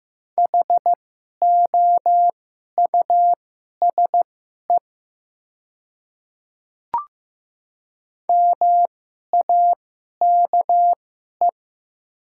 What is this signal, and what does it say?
Morse code, heard as a keyed tone, about 15 wpm, spaced as for 11 wpm, 700 Hz. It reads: HOUSE MAKE
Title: 2x Character Spacing - Top 300 Words - 15wpm